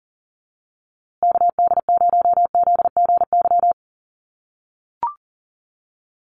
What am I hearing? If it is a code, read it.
Morse code, heard as a keyed tone, about 40 wpm, 700 Hz. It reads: KD0ZGY
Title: USA Call Signs Repeated 3x 40wpm